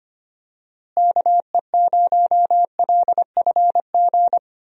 Morse code, heard as a keyed tone, about 25 wpm, 700 Hz. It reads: KE0LFG
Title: USA Call Signs - 6 Characters - Repeated 2x - 25wpm